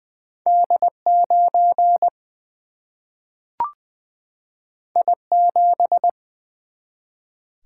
Morse code, heard as a keyed tone, about 20 wpm, 700 Hz. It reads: D 9 I 7
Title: Single-Letter Single-Number 20wpm